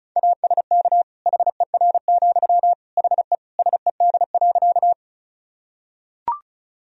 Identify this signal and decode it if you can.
Morse code, heard as a keyed tone, about 36 wpm, 700 Hz. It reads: ASK HER, HE SED.
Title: Sentences from Top 200 Words and Abbreviations 35wpm